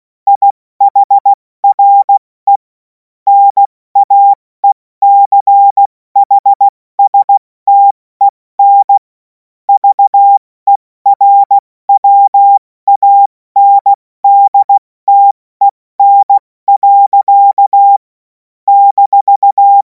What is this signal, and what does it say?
Morse code, heard as a keyed tone, about 16 wpm, 800 Hz. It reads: IHRE NAECHSTEN VERWANDTEN. -
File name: Morsekurs / Witze / Witz 07. Juli 2026 70 F80